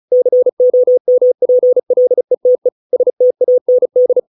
Morse code, heard as a keyed tone, about 34 wpm, 500 Hz. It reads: COMPLETE STAND